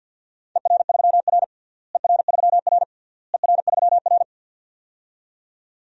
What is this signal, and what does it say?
Morse code, heard as a keyed tone, about 49 wpm, 700 Hz. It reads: ER3R ER3R ER3R
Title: DXCC-100 Call Signs - 4 Characters - Repeated 3x - 50wpm